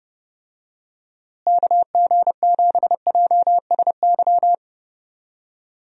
Morse code, heard as a keyed tone, about 30 wpm, 700 Hz. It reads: KG7JSY
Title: USA Call Signs - 6 Characters - Repeated 2x - 30wpm